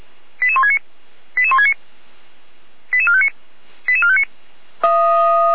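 Five-tone selective call (ZVEI), heard as quick runs of five tones, tones of about 70 ms. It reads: 88169 88169 88459 88459
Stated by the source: Received by SDR and recorded by the Feuerwehr Orlamünde fire brigade.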